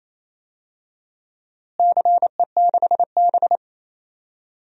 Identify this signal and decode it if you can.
Morse code, heard as a keyed tone, about 28 wpm, 700 Hz. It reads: CE6B
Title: DXCC-100 Call Signs - 4 Characters - Repeated 2x - 28wpm